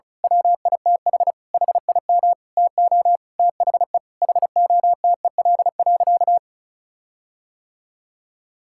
Morse code, heard as a keyed tone, about 34 wpm, 700 Hz. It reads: WITH HIM TO THE HOTEL.
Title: Sentences from Top 1600 Words 35wpm